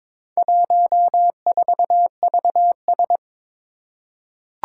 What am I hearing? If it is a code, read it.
Morse code, heard as a keyed tone, about 22 wpm, 700 Hz. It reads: 14VS
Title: Four Challenging Letter-Number Combinations 22wpm